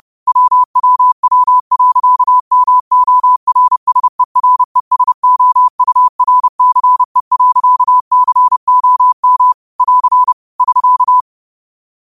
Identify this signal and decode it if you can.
Morse code, heard as a keyed tone, about 32 wpm, 1000 Hz. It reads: WWW.MORSERESOURCE.COM <AR> <SK>
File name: qotdfiles / 30wpm